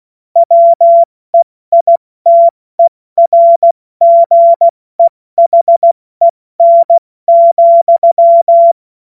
Morse code, heard as a keyed tone, about 16 wpm, 680 Hz. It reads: WEITERGEHEN,
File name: Morsekurs / Witze / Witz 15. November 2022 70 F80